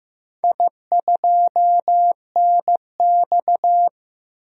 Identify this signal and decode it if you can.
Morse code, heard as a keyed tone, about 15 wpm, 700 Hz. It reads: I2NX
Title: DXCC-100 Call Signs - 4 Characters - 15wpm